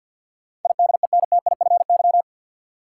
Morse code, heard as a keyed tone, about 53 wpm, 700 Hz. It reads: IDENTIFY
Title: Sentences from Top 900 Words - New Word Drill - 50wpm